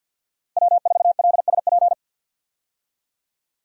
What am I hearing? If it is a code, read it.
Morse code, heard as a keyed tone, about 50 wpm, 700 Hz. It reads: W4LSP
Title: USA Call Signs - 5 Characters - 50wpm